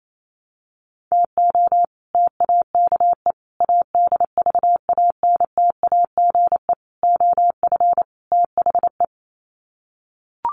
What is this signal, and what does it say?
Morse code, heard as a keyed tone, about 28 wpm, 700 Hz. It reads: TO TAKE ADVANTAGE OF THE
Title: Sets of 5 Words 28wpm